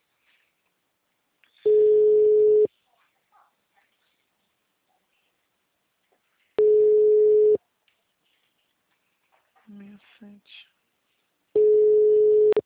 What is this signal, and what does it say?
Ringback tone (Europe, 425 Hz), with speech.